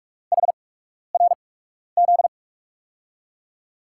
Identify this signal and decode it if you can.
Morse code, heard as a keyed tone, about 46 wpm, spaced as for 12 wpm, 700 Hz. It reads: HRZ
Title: USA Call Signs - 5 Characters - 8x Character Spacing - 45wpm